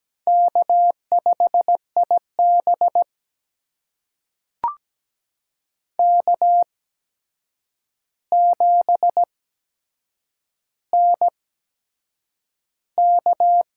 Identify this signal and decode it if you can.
Morse code, heard as a keyed tone, about 17 wpm, spaced as for 7 wpm, 700 Hz. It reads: K5IB K7NK
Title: USA Call Signs - 4 Characters - 8x Character Spacing - 17wpm